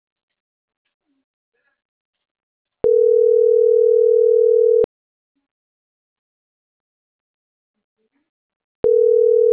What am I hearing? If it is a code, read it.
Ringback tone (North America), 440 + 480 Hz.